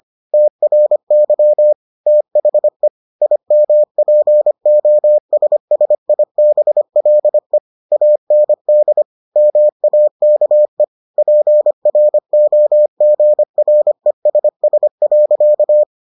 Morse code, heard as a keyed tone, about 25 wpm, 600 Hz. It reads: TRY THE IMPOSSIBLE AND MAKE PROGRESS.